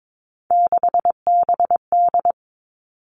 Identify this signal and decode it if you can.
Morse code, heard as a keyed tone, about 22 wpm, 700 Hz. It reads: 6BD
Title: Three Challenging Letter-Number Combinations 22wpm